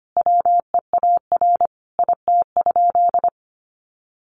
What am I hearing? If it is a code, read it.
Morse code, heard as a keyed tone, about 25 wpm, 700 Hz. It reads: WEAR IT?
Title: Sentences from Top 1500 Words 25wpm